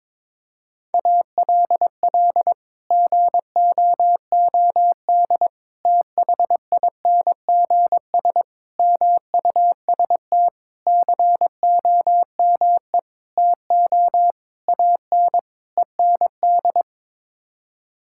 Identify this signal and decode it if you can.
Morse code, heard as a keyed tone, about 22 wpm, 700 Hz. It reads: ALL GOOD THINGS MUST COME TO AN END
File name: practice / 22wpm / Sets of 8 Words - 22wpm